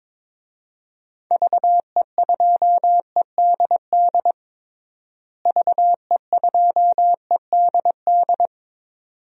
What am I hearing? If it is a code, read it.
Morse code, heard as a keyed tone, about 22 wpm, 700 Hz. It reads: VE2EDD VE2EDD
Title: DXCC-100 Call Signs - 6 Characters - Repeated 2x - 22wpm